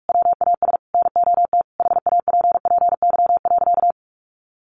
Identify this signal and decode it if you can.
Morse code, heard as a keyed tone, about 44 wpm, 700 Hz. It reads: WAS NOT HAPPY.
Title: Sentences from Top 1400 Words 45wpm